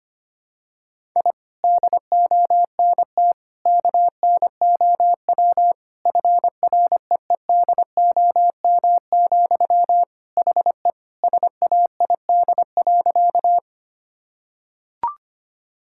Morse code, heard as a keyed tone, about 25 wpm, 700 Hz. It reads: I DONT KNOW FREEDOM, HE SAID.
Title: Sentences from Top 1000 Words - Review Part 5 - 25wpm